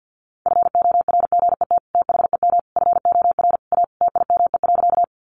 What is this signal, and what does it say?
Morse code, heard as a keyed tone, about 49 wpm, 700 Hz. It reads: FORGET THEM FOR A TIME.
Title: Sentences from Top 1200 Words 50wpm